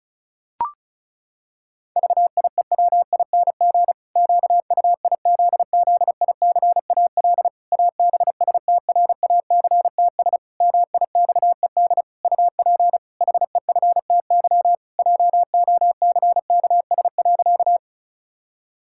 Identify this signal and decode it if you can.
Morse code, heard as a keyed tone, about 36 wpm, 700 Hz. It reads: VIEWING QUIZZICAL ABSTRACTS MIXED UP HEFTY JOCKS.
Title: Pangram Challenge v2 - 35wpm